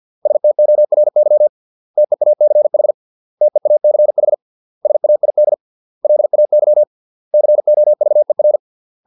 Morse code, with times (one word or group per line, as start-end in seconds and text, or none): 0.3-1.5 STORY
2.0-2.9 TEACH
3.4-4.3 TEACH
4.8-5.5 SAID
6.0-6.8 LAY
7.3-8.6 COVER